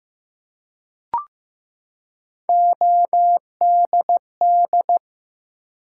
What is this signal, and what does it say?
Morse code, heard as a keyed tone, about 15 wpm, 700 Hz. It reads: ODD
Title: Three Letter Words - 15wpm